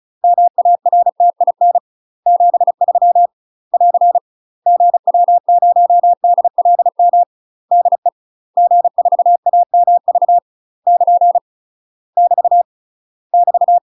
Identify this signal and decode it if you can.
Morse code, heard as a keyed tone, about 36 wpm, 700 Hz. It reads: MARTIN 73 <AR> GW0DLM DE G4AMV <KN> <BT> <BT>
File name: UK MASTER QSOs TRAINING / 35WPM / UK MASTER QSOs 0013